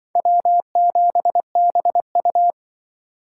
Morse code, heard as a keyed tone, about 24 wpm, 700 Hz. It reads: W7BU
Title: Speed Racing - USA 1x2 Call Signs 20wpm